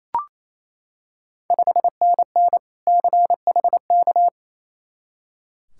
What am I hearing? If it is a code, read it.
Morse code, heard as a keyed tone, about 28 wpm, 700 Hz. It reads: 5NN CHK
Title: Ontario QSO Party Exchange 28wpm